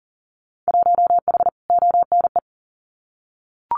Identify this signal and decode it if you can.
Morse code, heard as a keyed tone, about 37 wpm, 700 Hz. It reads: 1H ONE
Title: Winter Field Day Exchanges 40wpm